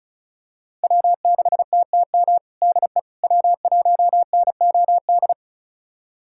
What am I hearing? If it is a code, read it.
Morse code, heard as a keyed tone, about 36 wpm, 700 Hz. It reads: W6TTM DE W1NOD
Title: USA Callsign DE Callsign - 5 Characters - 35wpm